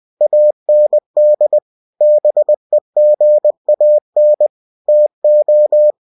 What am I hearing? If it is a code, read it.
Morse code, heard as a keyed tone, about 20 wpm, 600 Hz. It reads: AND BEGAN TO
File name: cw-books / winnie-the-pooh / winnie-20wpm-page-0107